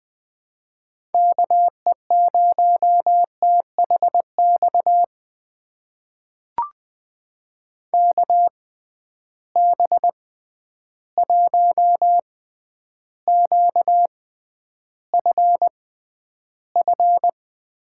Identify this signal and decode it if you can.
Morse code, heard as a keyed tone, about 20 wpm, spaced as for 10 wpm, 700 Hz. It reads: KE0THX KB1QFF
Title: USA Call Signs - 6 Characters - 6x Character Spacing - 20wpm